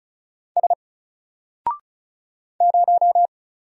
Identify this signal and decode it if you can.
Morse code, heard as a keyed tone, about 35 wpm, 700 Hz. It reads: S 0
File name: Single-Letter-Number-35wpm